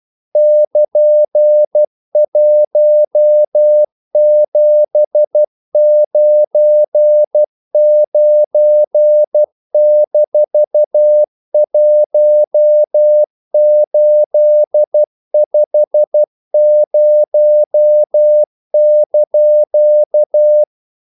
Morse code, heard as a keyed tone, about 12 wpm, 600 Hz. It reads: <KN>1799-1850)
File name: CW / Practice / EnglishQuotations0312